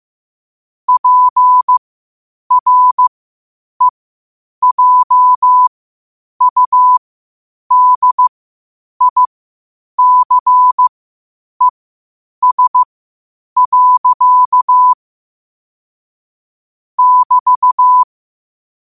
Morse code, heard as a keyed tone, about 15 wpm, spaced as for 8 wpm, 1000 Hz. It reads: PREJUDICES. <BT>
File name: qotdfiles / 8wpm